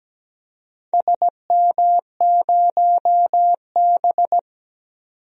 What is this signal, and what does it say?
Morse code, heard as a keyed tone, about 17 wpm, 700 Hz. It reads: SM0B
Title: DXCC-100 Call Signs - 4 Characters - 17wpm